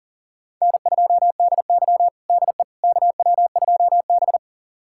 Morse code, heard as a keyed tone, about 37 wpm, 700 Hz. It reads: N2DY DE KW2B
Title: USA Callsign DE Callsign - 4 Characters - 40wpm